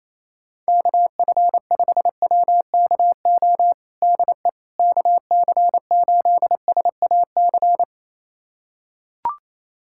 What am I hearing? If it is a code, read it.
Morse code, heard as a keyed tone, about 28 wpm, 700 Hz. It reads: KF5WKO DE KC8SAC
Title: USA Callsign DE Callsign 28wpm